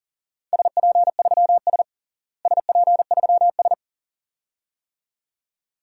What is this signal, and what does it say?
Morse code, heard as a keyed tone, about 40 wpm, 700 Hz. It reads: SP3S SP3S
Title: DXCC-100 Call Signs - 4 Characters - Repeated 2x - 40wpm